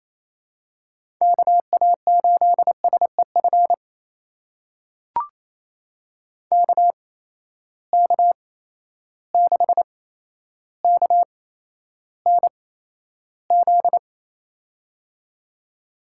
Morse code, heard as a keyed tone, about 29 wpm, spaced as for 12 wpm, 700 Hz. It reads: KA8SEF KK6KNZ
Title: USA Call Signs - 6 Characters - 8x Character Spacing - 28wpm